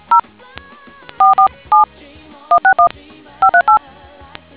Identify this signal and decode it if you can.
DTMF, heard as telephone touch tones, tones of about 110 ms. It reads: 0 47 161 538